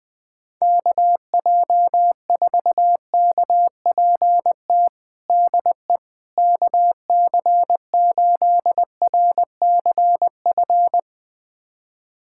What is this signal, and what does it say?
Morse code, heard as a keyed tone, about 20 wpm, 700 Hz. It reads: KJ4KPT DE KC8RCF
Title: USA Callsign DE Callsign - 6 Characters - 20wpm